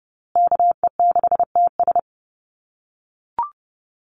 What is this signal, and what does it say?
Morse code, heard as a keyed tone, about 30 wpm, 700 Hz. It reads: KE6TS
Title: USA Call Signs Repeated 3x 30wpm